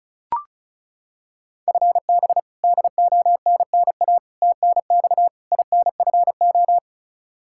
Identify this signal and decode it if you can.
Morse code, heard as a keyed tone, about 34 wpm, 700 Hz. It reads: FB DONNA TNX INFO
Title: Practice QSO - Exchange 3 - More Info - 35wpm